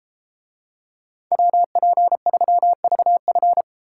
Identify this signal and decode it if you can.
Morse code, heard as a keyed tone, about 32 wpm, 700 Hz. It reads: WP3VF